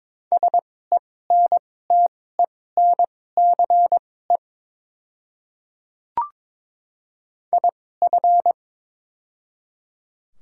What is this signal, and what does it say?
Morse code, heard as a keyed tone, about 22 wpm, spaced as for 16 wpm, 700 Hz. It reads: SENTENCE IF
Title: 2x Character Spacing - Top 200 Words - 22wpm